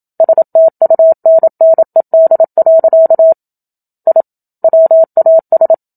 Morse code, heard as a keyed tone, about 27 wpm, 650 Hz. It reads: STUNNED. I WAS
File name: Book Two / 01-Chapter 1 - 27 WPM